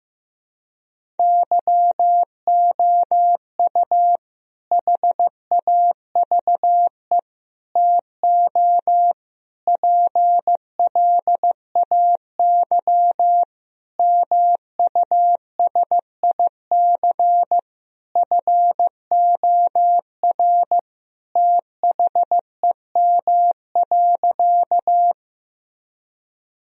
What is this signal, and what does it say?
Morse code, heard as a keyed tone, about 15 wpm, 700 Hz. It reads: YOU HAVE TO PLAY MUSIC FOR THEM.
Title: Sentences from Top 1000 Words - Review Part 1 - 15wpm